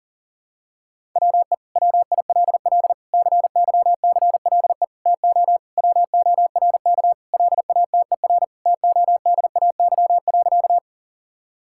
Morse code, heard as a keyed tone, about 40 wpm, 700 Hz. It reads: WE WILL CYCLE TO WORK LATER TODAY.
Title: Sentences from Top 1000 Words - Review Part 6 - 40wpm